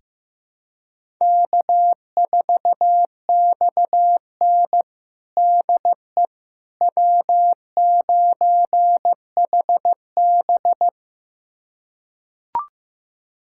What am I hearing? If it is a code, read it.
Morse code, heard as a keyed tone, about 16 wpm, 700 Hz. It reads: K4XN DE W9HB
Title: USA Callsign DE Callsign - 4 Characters - 15wpm